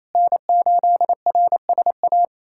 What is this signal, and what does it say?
Morse code, heard as a keyed tone, about 28 wpm, 700 Hz. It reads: N8RSA
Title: USA Callsign DE Callsign - 5 Characters - 28wpm